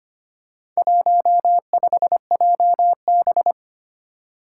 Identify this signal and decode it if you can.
Morse code, heard as a keyed tone, about 25 wpm, 700 Hz. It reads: 15JB
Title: Four Challenging Letter-Number Combinations 25wpm